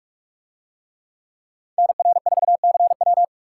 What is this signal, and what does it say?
Morse code, heard as a keyed tone, about 44 wpm, 700 Hz. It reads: NR4CW